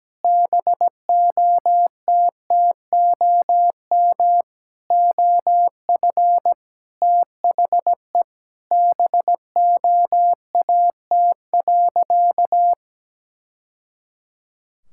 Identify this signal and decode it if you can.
Morse code, heard as a keyed tone, about 17 wpm, 700 Hz. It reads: BOTTOM OF THE BOAT.
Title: Sentences from Top 1000 Words - Review Part 6 - 17wpm